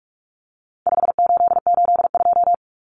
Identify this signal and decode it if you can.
Morse code, heard as a keyed tone, about 44 wpm, 700 Hz. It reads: 5882
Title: Numbers - 4-Digit 45wpm